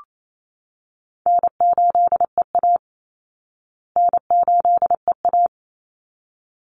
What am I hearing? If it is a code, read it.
Morse code, heard as a keyed tone, about 28 wpm, 700 Hz. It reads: N8EA N8EA
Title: Easy Call Signs Repeated 2x - 28wpm